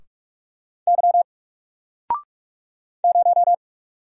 Morse code, heard as a keyed tone, about 44 wpm, 700 Hz. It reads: Y 0
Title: Single Character (Letter:Number:Punctuation:Prosign) 45wpm